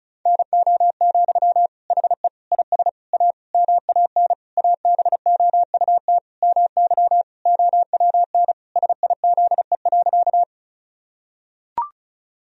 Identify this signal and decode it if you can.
Morse code, heard as a keyed tone, about 36 wpm, 700 Hz. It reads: NO, HE IS A MAN ABOUT MY OWN SIZE.
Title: Sentences from Top 1000 Words - Review Part 2 - 35wpm